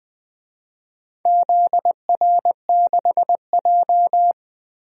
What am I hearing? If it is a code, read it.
Morse code, heard as a keyed tone, about 21 wpm, 700 Hz. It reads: ZR6J
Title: DXCC-100 Call Signs - 4 Characters - 20wpm